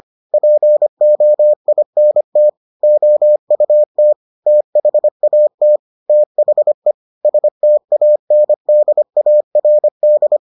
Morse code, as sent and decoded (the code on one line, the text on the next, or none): .--. --- .. -. - / --- ..- - / - .... .- - / - .... / ... - .- -. -.. .- .-. -..
POINT OUT THAT THE STANDARD